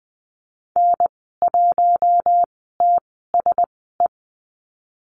Morse code, heard as a keyed tone, about 20 wpm, spaced as for 14 wpm, 700 Hz. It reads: N1TSE
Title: 082 Lesson 14g - Number 2 - Call Signs 20wpm